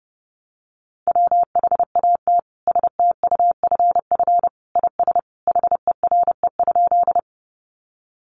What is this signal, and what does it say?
Morse code, heard as a keyed tone, about 30 wpm, 700 Hz. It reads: WHAT STUFF IS HERE?
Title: Sentences from Top 1400 Words 30wpm